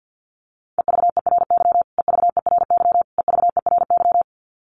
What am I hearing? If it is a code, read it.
Morse code, heard as a keyed tone, about 48 wpm, 700 Hz. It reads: EVERY EVERY EVERY
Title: Top 200 Words Repeated 50wpm